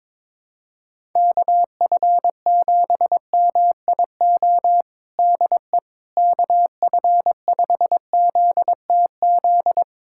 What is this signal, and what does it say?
Morse code, heard as a keyed tone, about 22 wpm, 700 Hz. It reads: KF7MIO DE KF5ZTZ
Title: USA Callsign DE Callsign 22wpm